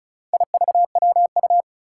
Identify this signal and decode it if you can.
Morse code, heard as a keyed tone, about 36 wpm, 700 Hz. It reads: IVWU
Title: Four Challenging Letter-Number Combinations 35wpm